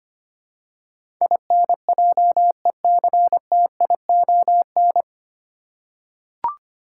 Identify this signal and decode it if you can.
Morse code, heard as a keyed tone, about 25 wpm, 700 Hz. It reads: INJECTION
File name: practice / 25wpm / 9 Letter Words - First 3 Letters Given - 25wpm